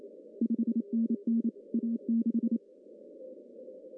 Morse code, heard as a keyed tone, about 27 wpm, 230 Hz. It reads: 5NN A6
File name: CW / cqww-cw-k5tr